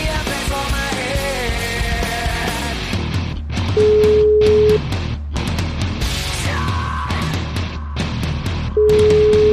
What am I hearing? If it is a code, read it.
Ringback tone (Europe, 425 Hz), with noise.